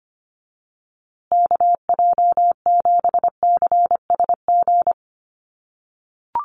Morse code, heard as a keyed tone, about 25 wpm, 700 Hz. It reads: KJ7CSG
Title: USA Call Signs Repeated 2x 25wpm